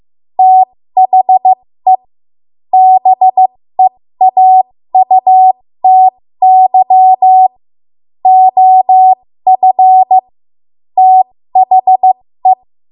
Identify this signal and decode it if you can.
Morse code, heard as a keyed tone, about 15 wpm, spaced as for 13 wpm, 750 Hz. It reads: THE BEAUTY OF THE
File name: Archive / 13 WPM / 13WPM